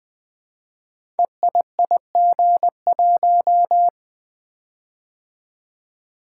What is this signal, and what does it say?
Morse code, heard as a keyed tone, about 20 wpm, 700 Hz. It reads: EIIG1
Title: Code Groups - 5 Letter:Numbers - 20wpm